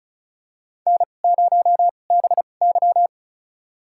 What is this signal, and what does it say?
Morse code, heard as a keyed tone, about 34 wpm, spaced as for 25 wpm, 700 Hz. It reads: N0BY
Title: USA Call Signs - 4 Characters - 2x Character Spacing - 35wpm